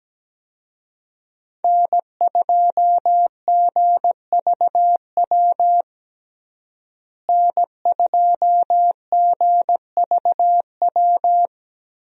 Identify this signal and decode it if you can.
Morse code, heard as a keyed tone, about 17 wpm, 700 Hz. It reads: N2GVW N2GVW
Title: USA Call Signs - 5 Characters - Repeated 3x - 17wpm